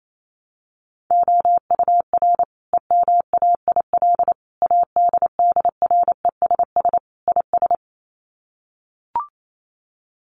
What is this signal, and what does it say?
Morse code, heard as a keyed tone, about 28 wpm, 700 Hz. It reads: OUR EMAIL ADDRESS IS
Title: Sets of 5 Words Encore2 28wpm